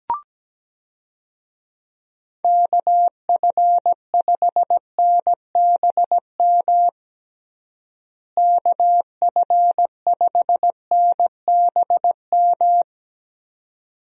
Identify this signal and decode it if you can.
Morse code, heard as a keyed tone, about 17 wpm, 700 Hz. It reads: KF5NBM KF5NBM
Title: USA Call Signs - 6 Characters - Repeated 2x - 17wpm